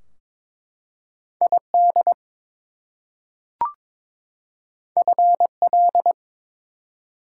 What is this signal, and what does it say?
Morse code, heard as a keyed tone, about 22 wpm, 700 Hz. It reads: ID FL
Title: US States and Canadian Provinces 22wpm